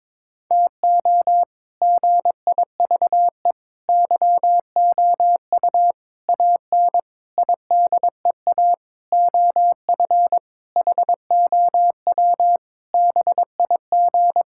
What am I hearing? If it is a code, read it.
Morse code, heard as a keyed tone, about 22 wpm, 700 Hz. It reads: TO GIVE YOU AN IDEA OF HOW BIG
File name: Sentences with 7-Word Phrases - 22wpm